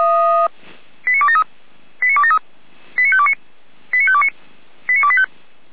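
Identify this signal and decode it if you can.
Five-tone selective call (ZVEI), heard as quick runs of five tones, tones of about 70 ms. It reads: 89273 89273 89529 89529 89285
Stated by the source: ZVEI five-tone selective call received on SDR.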